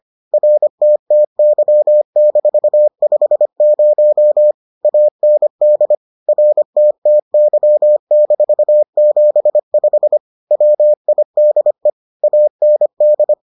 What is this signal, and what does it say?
Morse code, heard as a keyed tone, about 25 wpm, 600 Hz. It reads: RTTY-50 AND RTTY-75 WIDE AND